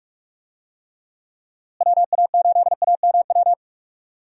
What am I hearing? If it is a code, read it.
Morse code, heard as a keyed tone, about 44 wpm, 700 Hz. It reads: WA8AMW
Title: USA Call Signs - 6 Characters - Repeated 3x - 45wpm